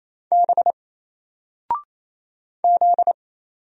Morse code, heard as a keyed tone, about 27 wpm, 700 Hz. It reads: B Z
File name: Single-Letter-Number-28wpm